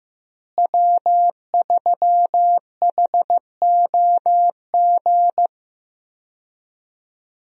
Morse code, heard as a keyed tone, about 16 wpm, 700 Hz. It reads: W3HOG
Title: USA Call Signs - 5 Characters - Repeated 3x - 15wpm